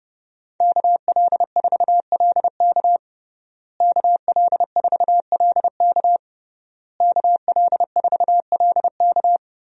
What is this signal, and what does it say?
Morse code, heard as a keyed tone, about 32 wpm, 700 Hz. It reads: KL4LK KL4LK KL4LK
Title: USA Call Signs - 5 Characters - Repeated 3x - 30wpm